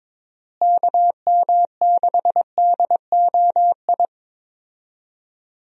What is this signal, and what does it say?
Morse code, heard as a keyed tone, about 22 wpm, 700 Hz. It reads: KM6DOI
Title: USA Call Signs - 6 Characters - Repeated 3x - 22wpm